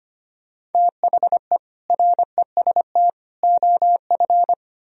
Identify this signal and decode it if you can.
Morse code, heard as a keyed tone, about 33 wpm, 700 Hz. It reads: THE REST OF